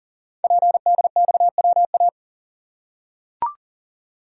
Morse code, heard as a keyed tone, about 40 wpm, 690 Hz. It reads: PDXWA
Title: Code Groups - 5 Letter:Numbers - 40wpm